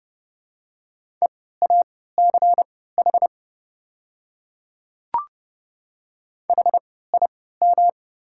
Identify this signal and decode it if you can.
Morse code, heard as a keyed tone, about 30 wpm, spaced as for 17 wpm, 700 Hz. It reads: EACH HIM
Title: 3x Character Spacing - Top 100 Words - 30wpm